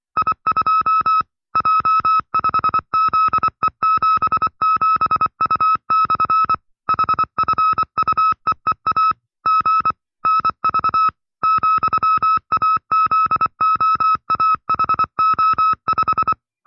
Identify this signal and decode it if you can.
Morse code, heard as a keyed tone, about 24 wpm, 1310 Hz. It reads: I2 J5ZE77U/ HFUEEA G NV ,AZOAHO5
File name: FreeFiles / CW-practice / 20WPM-side2